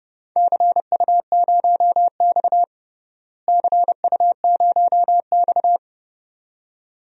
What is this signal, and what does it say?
Morse code, heard as a keyed tone, about 32 wpm, 700 Hz. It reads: CU0X CU0X
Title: DXCC-100 Call Signs - 4 Characters - Repeated 2x - 30wpm